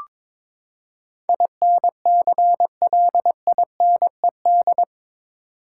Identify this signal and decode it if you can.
Morse code, heard as a keyed tone, about 22 wpm, 700 Hz. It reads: INCLINED
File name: practice / 22wpm / Eight Letter Words - 22wpm